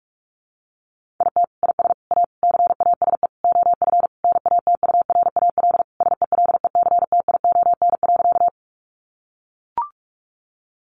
Morse code, heard as a keyed tone, about 44 wpm, 700 Hz. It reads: IT IS A CASE OF NATURAL SELECTION.
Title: Sentences from Top 1400 Words 45wpm